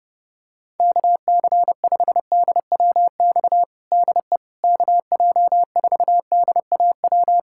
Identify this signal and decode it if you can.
Morse code, heard as a keyed tone, about 30 wpm, 700 Hz. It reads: KC5DWX DE KJ4DAW
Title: USA Callsign DE Callsign - 6 Characters - 30wpm